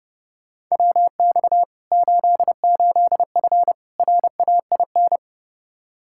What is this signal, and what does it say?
Morse code, heard as a keyed tone, about 32 wpm, 700 Hz. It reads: WX 88F RAIN
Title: Practice QSO - Exchange 2 - Info - 30wpm